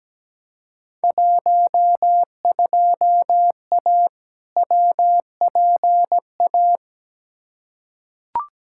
Morse code, heard as a keyed tone, about 17 wpm, 700 Hz. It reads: 12A WPA
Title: Field Day Exchanges Encore - 17wpm